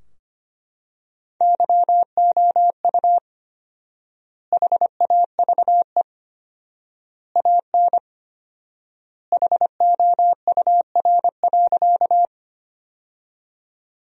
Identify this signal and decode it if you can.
Morse code, heard as a keyed tone, about 25 wpm, 700 Hz. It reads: YOU HAVE AN HOUR.
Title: 4x Word-Spacing - Sentences from Top 400 Words - 25wpm